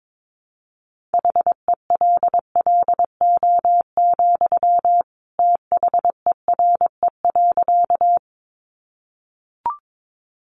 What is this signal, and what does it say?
Morse code, heard as a keyed tone, about 22 wpm, 700 Hz. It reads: HELLO, THERE.
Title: Sentences from Top 1100 Words 22wpm